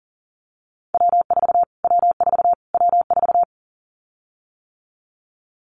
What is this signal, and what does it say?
Morse code, heard as a keyed tone, about 40 wpm, 700 Hz. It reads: W4 W4 W4
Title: Callsign Prefixes for Top 25 DXCC - Repeated 3x - 40wpm